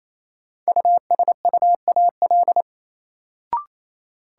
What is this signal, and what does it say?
Morse code, heard as a keyed tone, about 28 wpm, 700 Hz. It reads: USUAL
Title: Sentences from Top 400 Words - New Word Drill - 28wpm